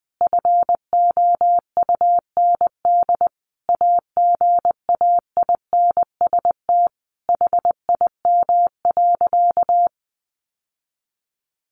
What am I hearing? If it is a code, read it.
Morse code, heard as a keyed tone, about 20 wpm, 700 Hz. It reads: FOUND AGAINST HIM.
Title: Sentences from Top 1300 Words 20wpm